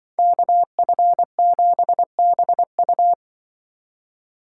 Morse code, heard as a keyed tone, about 24 wpm, 700 Hz. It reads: KF7BU
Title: Speed Racing - USA 2x2 Call Signs 20wpm